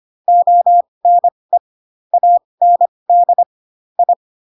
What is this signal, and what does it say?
Morse code, heard as a keyed tone, about 25 wpm, spaced as for 20 wpm, 700 Hz. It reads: ONE AND I
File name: Pinocchio / Pinocchio ch7 15 25 ik1hge